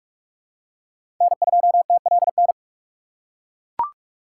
Morse code, heard as a keyed tone, about 42 wpm, 700 Hz. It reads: N2TLN
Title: USA Call Signs - 5 Characters - 4x Character Spacing - 45wpm